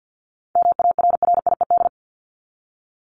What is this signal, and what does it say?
Morse code, heard as a keyed tone, about 49 wpm, 700 Hz. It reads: MARRIED
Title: ED Suffix - Top 100 Words with Base Words 50wpm